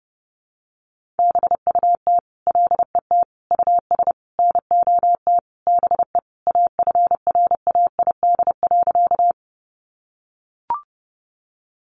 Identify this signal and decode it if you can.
Morse code, heard as a keyed tone, about 30 wpm, 700 Hz. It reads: BUT LET US NOT BE AFRAID.
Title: Sentences from Top 1500 Words 30wpm